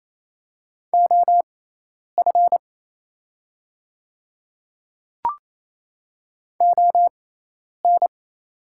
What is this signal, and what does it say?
Morse code, heard as a keyed tone, about 28 wpm, spaced as for 10 wpm, 690 Hz. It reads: OF ON
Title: 6x Character Spacing - Top 100 Words - 28wpm